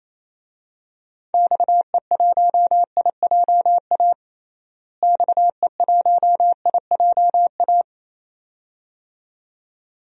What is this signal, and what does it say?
Morse code, heard as a keyed tone, about 28 wpm, 700 Hz. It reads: XE1IJA XE1IJA
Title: DXCC-100 Call Signs - 6 Characters - Repeated 2x - 28wpm